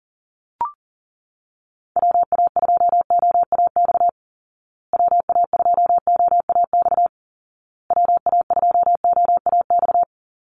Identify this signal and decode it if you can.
Morse code, heard as a keyed tone, about 40 wpm, 700 Hz. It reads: WA2OAX WA2OAX WA2OAX
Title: Call Signs Repeated 2x - 40wpm